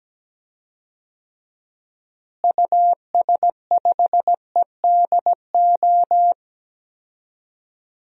Morse code, heard as a keyed tone, about 17 wpm, 700 Hz. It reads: US5EDO
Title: DXCC-100 Call Signs - 6 Characters - Repeated 2x - 17wpm